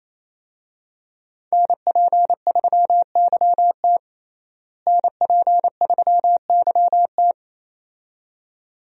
Morse code, heard as a keyed tone, about 28 wpm, 700 Hz. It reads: NP3YT NP3YT
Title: USA Call Signs - 5 Characters - Repeated 2x - 28wpm